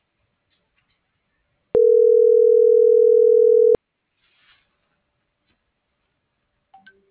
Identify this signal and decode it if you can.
Ringback tone (North America), 440 + 480 Hz.
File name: CENTRAL MAYAB / out jorge.armando